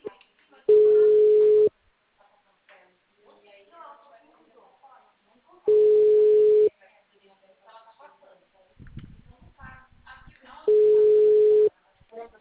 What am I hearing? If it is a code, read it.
Ringback tone (Europe, 425 Hz), with speech.
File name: out- -2001- - - .35532